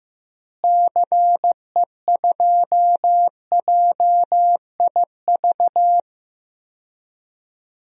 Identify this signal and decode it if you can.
Morse code, heard as a keyed tone, about 15 wpm, 700 Hz. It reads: CE2JIV
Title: DXCC-100 Call Signs - 6 Characters - Repeated 3x - 15wpm